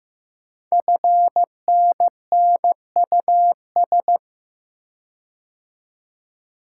Morse code, heard as a keyed tone, about 15 wpm, 700 Hz. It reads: FNNUS
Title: Code Groups - 5 Letters - 15wpm